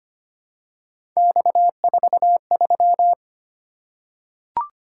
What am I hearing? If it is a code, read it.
Morse code, heard as a keyed tone, about 25 wpm, 700 Hz. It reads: X43
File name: Three Letter-Number Combinations - 25wpm